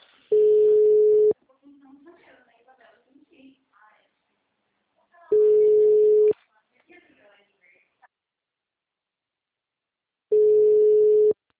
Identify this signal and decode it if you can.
Ringback tone (Europe, 425 Hz), with speech.